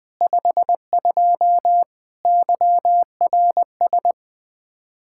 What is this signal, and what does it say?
Morse code, heard as a keyed tone, about 20 wpm, 700 Hz. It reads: 52 YRS
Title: Speed-Racing Common QSO Phrases - 17wpm